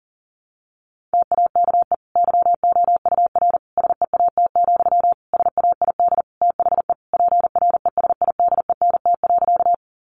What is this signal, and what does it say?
Morse code, heard as a keyed tone, about 40 wpm, 700 Hz. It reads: TAKE YOUR SEAT, SAID THE PRESIDENT.
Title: Sentences from Top 1500 Words 40wpm